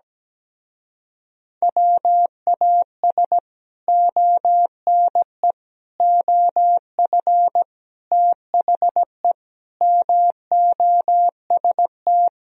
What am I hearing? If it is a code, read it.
Morse code, heard as a keyed tone, about 17 wpm, 700 Hz. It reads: WAS ONE OF THE MOST
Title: Speed-Racing - Sets of 5 Words 17wpm